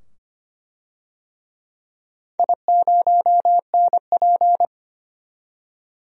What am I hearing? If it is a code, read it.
Morse code, heard as a keyed tone, about 25 wpm, 700 Hz. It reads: I0NP
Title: DXCC-100 Call Signs - 4 Characters - 25wpm